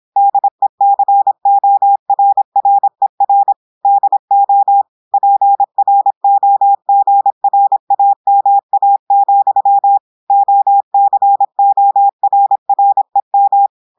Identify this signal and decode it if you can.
Morse code, heard as a keyed tone, about 26 wpm, 800 Hz. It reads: DECORRER DO PROGRAMA, OCORREM
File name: portugais texte en clair 2 vitesse 26 mots minutes